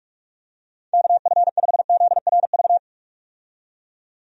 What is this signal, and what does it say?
Morse code, heard as a keyed tone, about 46 wpm, 700 Hz. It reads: KF5ZRV
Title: USA Call Signs - 6 Characters - 45wpm